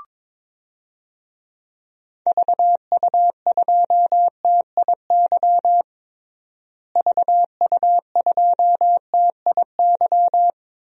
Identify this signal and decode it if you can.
Morse code, heard as a keyed tone, about 22 wpm, 700 Hz. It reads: VU2TIY VU2TIY
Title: DXCC-100 Call Signs - 6 Characters - Repeated 3x - 22wpm